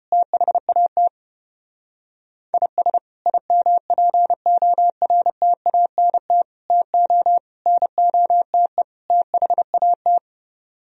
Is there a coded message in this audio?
Yes: Morse code from 0.1 to 10.2 s, heard as a keyed tone, about 34 wpm, 700 Hz. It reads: THAT IS IMPORTANT TO NOTE THAT